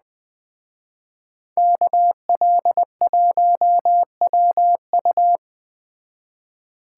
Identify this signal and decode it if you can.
Morse code, heard as a keyed tone, about 20 wpm, 700 Hz. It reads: KL1WU
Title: Speed Racing - USA 2x2 Call Signs 17wpm